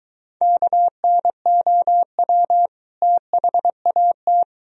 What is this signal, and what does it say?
Morse code, heard as a keyed tone, about 22 wpm, 700 Hz. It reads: KNOW THAT